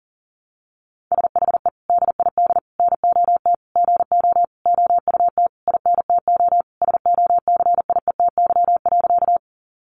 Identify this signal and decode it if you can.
Morse code, heard as a keyed tone, about 40 wpm, 700 Hz. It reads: SHE DID NOT GO OUT INTO SOCIETY.
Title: Sentences from Top 700 Words 40wpm